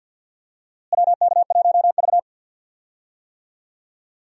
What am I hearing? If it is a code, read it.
Morse code, heard as a keyed tone, about 49 wpm, 700 Hz. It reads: WK1V
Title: USA Call Signs - 4 Characters - Repeated 2x - 50wpm